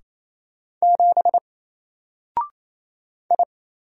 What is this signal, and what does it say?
Morse code, heard as a keyed tone, about 27 wpm, 700 Hz. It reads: 7 I
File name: Single-Letter-Number-28wpm